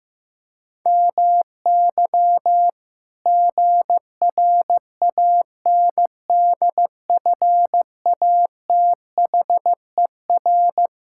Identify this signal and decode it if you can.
Morse code, heard as a keyed tone, about 15 wpm, 700 Hz. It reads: MY GRANDFATHER
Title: Pangram Challenge v2 - 15wpm